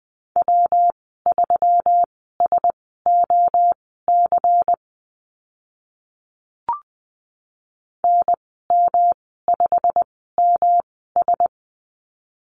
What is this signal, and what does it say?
Morse code, heard as a keyed tone, about 20 wpm, spaced as for 15 wpm, 700 Hz. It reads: W3SOC NM5MS
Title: 140 Lesson 23g - Letter f - Call Signs 20wpm